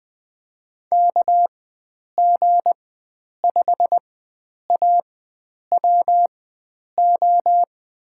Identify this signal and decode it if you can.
Morse code, heard as a keyed tone, about 20 wpm, spaced as for 9 wpm, 700 Hz. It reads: KG5AWO
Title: USA Call Signs - 6 Characters - 4x Character Spacing - 20wpm